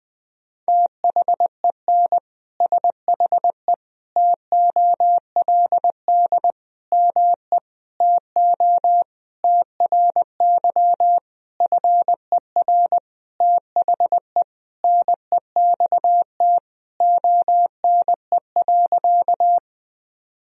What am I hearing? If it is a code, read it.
Morse code, heard as a keyed tone, about 20 wpm, 700 Hz. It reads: THEN SHE TOLD ME TO TRY FER THE NEXT ONE.
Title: Sentences from Top 400 Words and Abbreviations 20wpm